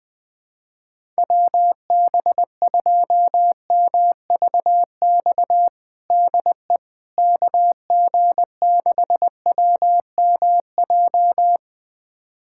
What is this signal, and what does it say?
Morse code, heard as a keyed tone, about 21 wpm, 700 Hz. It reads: WB2MVX DE KG6WMJ
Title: USA Callsign DE Callsign - 6 Characters - 20wpm